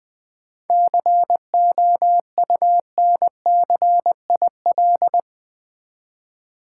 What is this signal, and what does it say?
Morse code, heard as a keyed tone, about 20 wpm, 700 Hz. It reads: COUNCIL
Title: 7 Letter Words - First 3 Letters Given - 20wpm